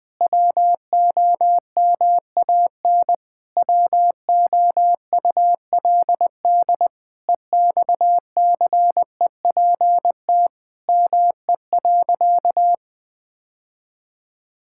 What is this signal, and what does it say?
Morse code, heard as a keyed tone, about 21 wpm, 700 Hz. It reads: WOMAN WOULD EXCEPT ME.
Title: Sentences from Top 1600 Words 20wpm